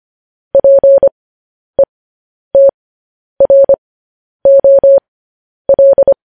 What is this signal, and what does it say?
Morse code, heard as a keyed tone, about 25 wpm, spaced as for 10 wpm, 550 Hz. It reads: PETROL
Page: news headlines in morse code at 10 wpm